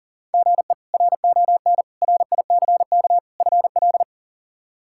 Morse code, heard as a keyed tone, about 40 wpm, 700 Hz. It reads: GE RON RICK FL
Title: K1USN SST QSO Practice - 40wpm